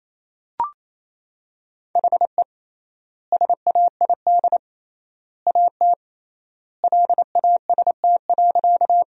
Morse code, heard as a keyed tone, about 28 wpm, 700 Hz. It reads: HE SAID AT LAST.
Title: 3x Word-Spacing - Sentences from Top 300 Words - 28wpm